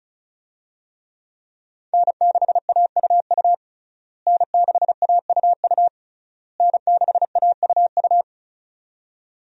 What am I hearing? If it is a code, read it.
Morse code, heard as a keyed tone, about 36 wpm, 700 Hz. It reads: N6AUU N6AUU N6AUU
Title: USA Call Signs - 5 Characters - Repeated 3x - 35wpm